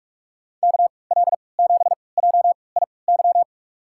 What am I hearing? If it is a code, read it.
Morse code, heard as a keyed tone, about 46 wpm, spaced as for 26 wpm, 700 Hz. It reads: KL7JIY
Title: USA Call Signs - 6 Characters - 3x Character Spacing - 45wpm